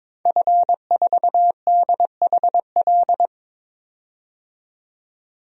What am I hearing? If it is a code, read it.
Morse code, heard as a keyed tone, about 22 wpm, 700 Hz. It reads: F4DHL
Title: DXCC-100 Call Signs - 5 Characters - 22wpm